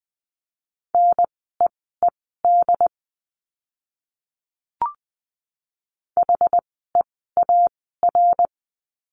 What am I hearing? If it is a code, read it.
Morse code, heard as a keyed tone, about 20 wpm, spaced as for 15 wpm, 700 Hz. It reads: NEED HEAR
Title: 206 Lesson 33f - Letter j - Words 20wpm